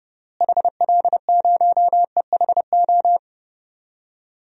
Morse code, heard as a keyed tone, about 30 wpm, 700 Hz. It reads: HL0EHO
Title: DXCC-100 Call Signs - 6 Characters - Repeated 3x - 30wpm